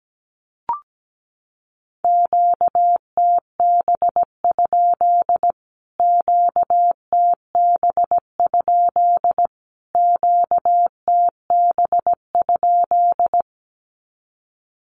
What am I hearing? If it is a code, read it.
Morse code, heard as a keyed tone, about 17 wpm, 700 Hz. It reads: QTB? QTB? QTB?
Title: Q-Codes Repeated 3x 17wpm